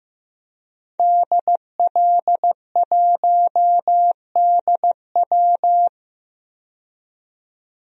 Morse code, heard as a keyed tone, about 15 wpm, 700 Hz. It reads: DL1DW
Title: DXCC-100 Call Signs - 5 Characters - 15wpm